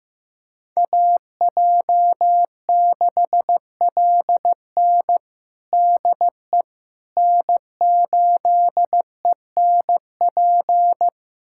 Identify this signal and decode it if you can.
Morse code, heard as a keyed tone, about 16 wpm, 700 Hz. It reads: AJ6LN DE N8ENP
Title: USA Callsign DE Callsign - 5 Characters - 15wpm